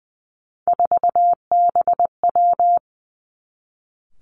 Morse code, heard as a keyed tone, about 20 wpm, 700 Hz. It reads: 4BW
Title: Three Challenging Letter-Number Combinations 20wpm